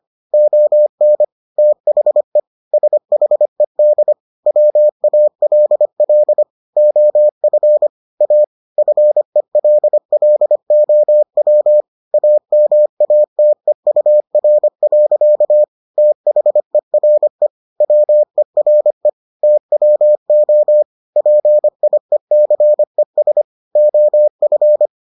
Morse code, heard as a keyed tone, about 25 wpm, 600 Hz. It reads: ON THE SHED WALL OF A FELLOW AMATEUR. THERE WERE TWO PIECES OF